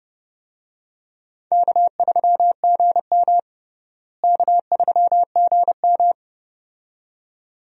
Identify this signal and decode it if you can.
Morse code, heard as a keyed tone, about 30 wpm, 700 Hz. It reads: K3GM K3GM
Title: USA Call Signs - 4 Characters - Repeated 2x - 30wpm